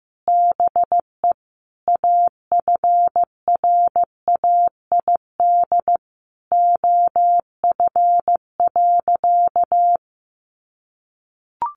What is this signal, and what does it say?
Morse code, heard as a keyed tone, about 15 wpm, 700 Hz. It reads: BE AFRAID OF.
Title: Sentences from Top 1500 Words 15wpm